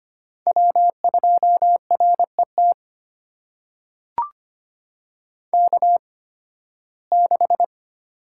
Morse code, heard as a keyed tone, about 25 wpm, spaced as for 16 wpm, 700 Hz. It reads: W2RET K6
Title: USA Call Signs - 8x Character Spacing - 25wpm